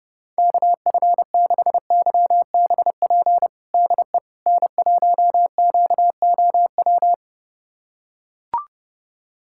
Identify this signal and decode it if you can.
Morse code, heard as a keyed tone, about 30 wpm, 690 Hz. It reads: KF6YBP DE N1QOW
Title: USA Callsign DE Callsign 30wpm